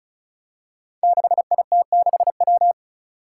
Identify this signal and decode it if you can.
Morse code, heard as a keyed tone, about 34 wpm, 700 Hz. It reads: 6IT6W
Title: Five Challenging Letter-Number Combinations 35wpm